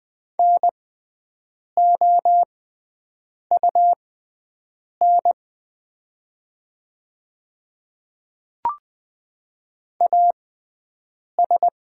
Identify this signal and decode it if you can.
Morse code, heard as a keyed tone, about 20 wpm, spaced as for 7 wpm, 690 Hz. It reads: NOUN AS
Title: 6x Character Spacing - Top 500 Words - 20wpm